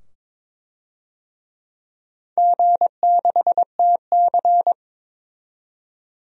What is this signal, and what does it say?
Morse code, heard as a keyed tone, about 22 wpm, 700 Hz. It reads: G6TC
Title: DXCC-100 Call Signs - 4 Characters - 22wpm